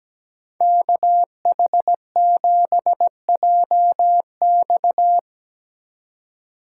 Morse code, heard as a keyed tone, about 19 wpm, 700 Hz. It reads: KH7JX
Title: Speed Racing - USA 2x2 Call Signs 17wpm